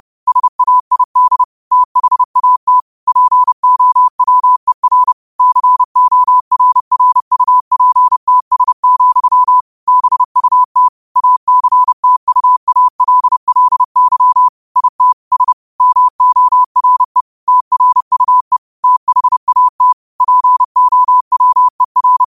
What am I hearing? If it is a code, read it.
Morse code, heard as a keyed tone, about 30 wpm, 1000 Hz. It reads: SAID THAT POWER CORRUPTS, BUT ACTUALLY IT S MORE TRUE THAT POWER
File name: qotdfiles / 30wpm